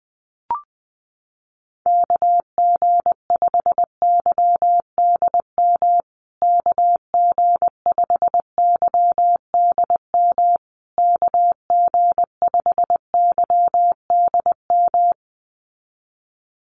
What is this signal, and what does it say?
Morse code, heard as a keyed tone, about 20 wpm, 700 Hz. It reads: KG5YDM KG5YDM KG5YDM
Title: USA Call Signs Repeated 3x 20wpm